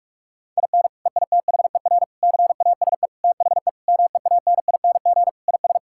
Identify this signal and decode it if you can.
Morse code, heard as a keyed tone, about 48 wpm, 700 Hz. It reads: IN EITHER CASE THE MEANING IS
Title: Sentences from Top 1000 Words - Review Part 6 - 45wpm